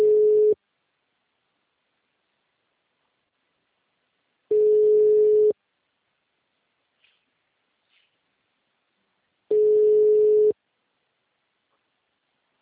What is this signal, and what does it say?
Ringback tone (Europe, 425 Hz).